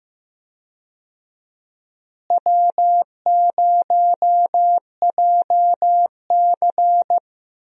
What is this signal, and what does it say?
Morse code, heard as a keyed tone, about 15 wpm, 700 Hz. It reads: W0JC
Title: USA Call Signs - 4 Characters - Repeated 3x - 15wpm